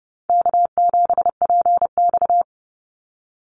Morse code, heard as a keyed tone, about 28 wpm, 700 Hz. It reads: K7PX
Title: NA SOTA Activator Callsigns 30wpm